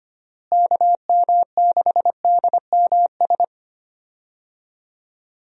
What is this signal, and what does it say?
Morse code, heard as a keyed tone, about 25 wpm, 700 Hz. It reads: KM6DMS
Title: USA Call Signs - 6 Characters - Repeated 3x - 25wpm